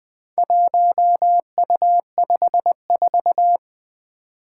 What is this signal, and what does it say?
Morse code, heard as a keyed tone, about 20 wpm, 700 Hz. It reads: 1U54
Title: Four Challenging Letter-Number Combinations 20wpm